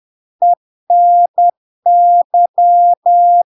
Morse code, heard as a keyed tone, about 10 wpm, 700 Hz. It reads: ENY
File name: LARK-FD-2022-Log-in-CW-10wpm-700-Hz